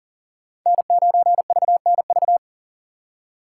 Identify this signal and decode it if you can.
Morse code, heard as a keyed tone, about 51 wpm, 700 Hz. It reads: N9VNV